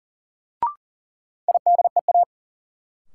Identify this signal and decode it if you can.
Morse code, heard as a keyed tone, about 40 wpm, 700 Hz. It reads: IDEA
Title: Sets of 3 Words - New Word Drill - 40wpm